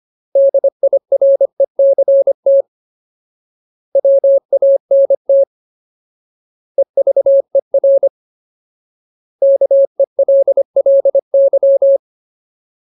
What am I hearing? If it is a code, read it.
Morse code, heard as a keyed tone, about 25 wpm, 550 Hz. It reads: DIRECT WANT EVER KELLY
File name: Ultimate Word List / 25WPM Ultimate